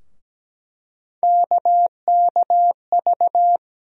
Morse code, heard as a keyed tone, about 17 wpm, 700 Hz. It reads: KKV
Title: USA Call Sign Postfixes - 3 Letters 17wpm